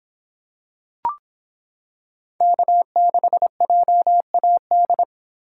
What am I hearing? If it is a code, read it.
Morse code, heard as a keyed tone, about 26 wpm, 700 Hz. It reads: K6JAD